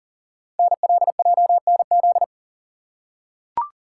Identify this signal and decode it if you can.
Morse code, heard as a keyed tone, about 42 wpm, 700 Hz. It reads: NLJNZ
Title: Code Groups - 5 Letters - 40wpm